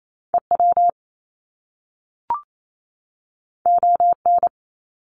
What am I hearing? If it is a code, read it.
Morse code, heard as a keyed tone, about 28 wpm, 700 Hz. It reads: EW ON
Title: Top Two Letter Combinations in Words 28wpm